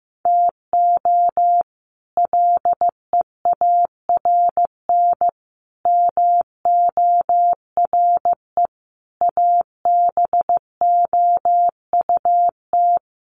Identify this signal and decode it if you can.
Morse code, heard as a keyed tone, about 15 wpm, 700 Hz. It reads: TO LEARN MORE ABOUT
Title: Sets of 5 Words Encore2 15wpm